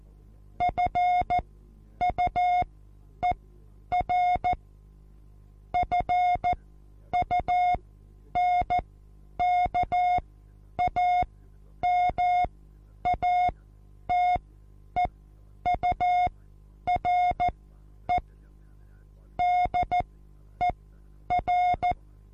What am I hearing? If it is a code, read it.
Morse code, heard as a keyed tone, about 14 wpm, spaced as for 9 wpm, 720 Hz. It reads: FUER FUNKAMATEURE DER
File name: Onlinelehrgang / morsen / DJ4UF Morsen23